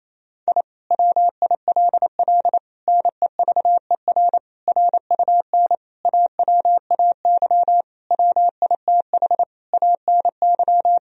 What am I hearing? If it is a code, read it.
Morse code, heard as a keyed tone, about 28 wpm, 700 Hz. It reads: I WILL NEVER RUN AWAY WITH ANY
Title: Sentences from Top 1000 Words - Review Part 1 - 28wpm